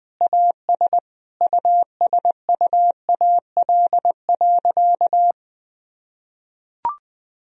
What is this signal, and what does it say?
Morse code, heard as a keyed tone, about 20 wpm, 700 Hz. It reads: AS USUAL.
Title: Sentences from Top 1000 Words - Review Part 2 - 20wpm